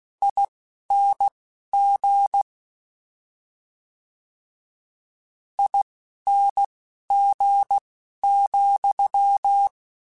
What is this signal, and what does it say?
Morse code, heard as a keyed tone, about 16 wpm, spaced as for 12 wpm, 800 Hz. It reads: ING ING,
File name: Combinations / Combinations-mp3s / Combinations@10